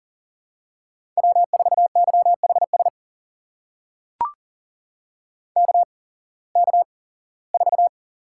Morse code, heard as a keyed tone, about 38 wpm, spaced as for 21 wpm, 700 Hz. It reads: W4YHS KK4
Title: USA Call Signs - 8x Character Spacing - 40wpm